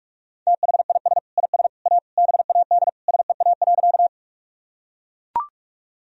Morse code, heard as a keyed tone, about 41 wpm, 700 Hz. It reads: THIS IS A BAD SEA.
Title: Sentences from Top 1000 Words - Review Part 1 - 45wpm